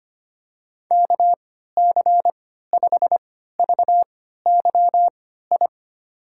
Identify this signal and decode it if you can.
Morse code, heard as a keyed tone, about 25 wpm, spaced as for 14 wpm, 700 Hz. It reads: KC5VYI
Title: USA Call Signs - 6 Characters - 3x Character Spacing - 25wpm